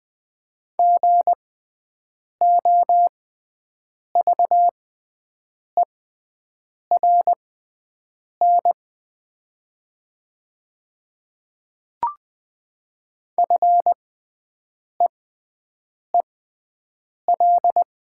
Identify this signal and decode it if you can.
Morse code, heard as a keyed tone, about 20 wpm, spaced as for 7 wpm, 700 Hz. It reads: GOVERN FEEL
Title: 6x Character Spacing - Top 500 Words - 20wpm